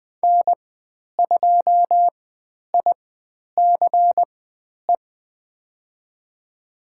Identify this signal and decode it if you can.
Morse code, heard as a keyed tone, about 20 wpm, spaced as for 10 wpm, 700 Hz. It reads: N2ICE
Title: 095 Lesson 16g - Letter c - Call Signs 20wpm